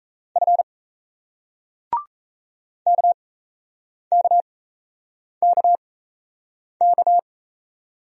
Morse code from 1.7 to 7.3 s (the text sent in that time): KKKK